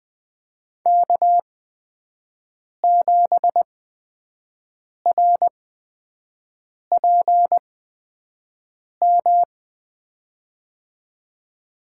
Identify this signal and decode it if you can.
Morse code, heard as a keyed tone, about 20 wpm, spaced as for 5 wpm, 700 Hz. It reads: K7RPM
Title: USA Call Signs - 8x Character Spacing - 20wpm